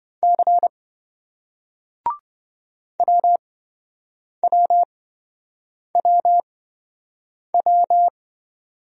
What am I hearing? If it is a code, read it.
Morse code, heard as a keyed tone, about 29 wpm, spaced as for 7 wpm, 700 Hz. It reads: C WWWW